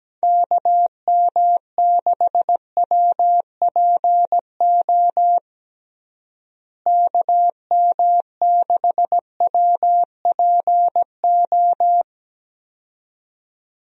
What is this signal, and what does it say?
Morse code, heard as a keyed tone, about 17 wpm, 700 Hz. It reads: KM6WPO KM6WPO
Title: USA Call Signs - 6 Characters - Repeated 2x - 17wpm